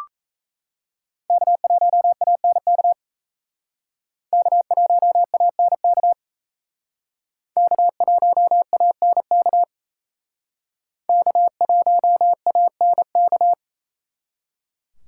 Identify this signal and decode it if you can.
Morse code, heard as a keyed tone, about 36 wpm, 700 Hz. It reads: K1ANK K1ANK K1ANK K1ANK